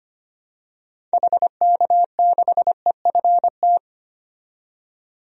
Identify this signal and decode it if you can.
Morse code, heard as a keyed tone, about 25 wpm, 700 Hz. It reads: HK6EFT
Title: DXCC-100 Call Signs - 6 Characters - 25wpm